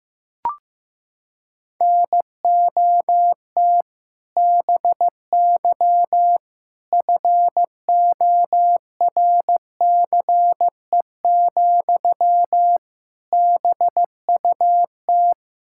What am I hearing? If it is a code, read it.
Morse code, heard as a keyed tone, about 15 wpm, 700 Hz. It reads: NOT BY FORCE, BUT
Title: Sentences from Top 1000 Words - Review Part 5 - 15wpm